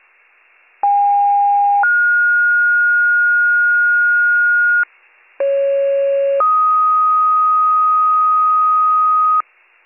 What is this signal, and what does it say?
Two-tone page, heard as a one-second tone, then a three-second tone, about 810 then 1430 Hz, about 550 then 1250 Hz, with noise. It is heard from 0.8 to 9.4 s.